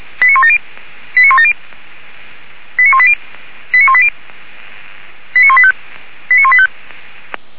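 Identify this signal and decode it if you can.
Five-tone selective call (ZVEI), heard as quick runs of five tones, tones of about 70 ms. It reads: 89170 89170 89180 89180 89185 89185
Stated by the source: ZVEI five-tone selective call received on SDR, fire brigade alerting.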